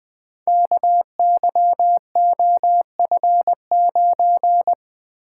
Morse code, heard as a keyed tone, about 20 wpm, 700 Hz. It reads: KYOF9
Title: Code Groups - 5 Letter:Numbers - 20wpm